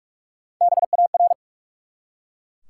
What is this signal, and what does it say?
Morse code, heard as a keyed tone, about 44 wpm, 700 Hz. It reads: BAR
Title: Three Letter Words - 45wpm